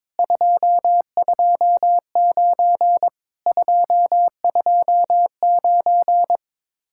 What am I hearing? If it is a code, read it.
Morse code, heard as a keyed tone, about 22 wpm, 700 Hz. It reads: 229 229
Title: Speed-Racing Common QSO Phrases - 22wpm